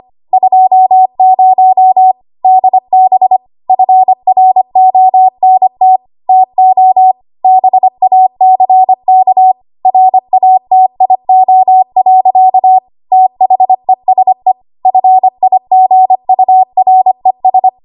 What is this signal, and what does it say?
Morse code, heard as a keyed tone, about 25 wpm, 750 Hz. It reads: 20 DB FRONT TO BACK RATIO. THESE FIGURES
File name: Archive / 25 WPM / 25WPM